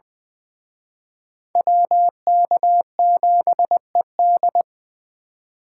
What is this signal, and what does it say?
Morse code, heard as a keyed tone, about 23 wpm, 700 Hz. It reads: WK7ED
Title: Speed Racing - USA 2x2 Call Signs 20wpm